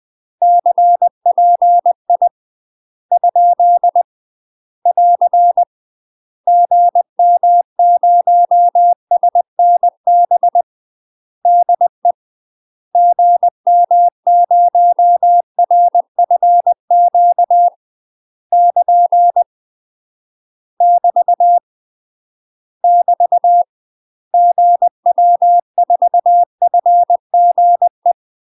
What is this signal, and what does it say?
Morse code, heard as a keyed tone, about 20 wpm, 700 Hz. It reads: CPI ? <AR> GM0SNB DE GM0RFQ <KN> <BT> <BT> GW4FGE
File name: UK MASTER QSOs TRAINING / 20WPM / UK MASTER QSOs 0060